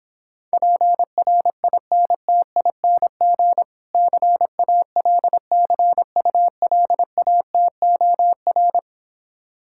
Morse code, heard as a keyed tone, about 26 wpm, 700 Hz. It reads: PRINTING CALCULATOR